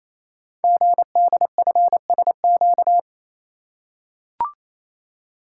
Morse code, heard as a keyed tone, about 27 wpm, 700 Hz. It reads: GDFSQ
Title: Code Groups - 5 Letters - 28wpm